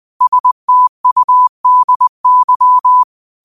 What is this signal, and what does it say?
Morse code, heard as a keyed tone, about 20 wpm, 1000 Hz. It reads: STUDY
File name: qotdfiles / 20wpm